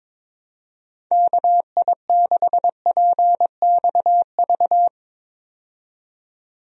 Morse code, heard as a keyed tone, about 22 wpm, 700 Hz. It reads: KI6PXV
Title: USA Call Signs - 6 Characters - 22wpm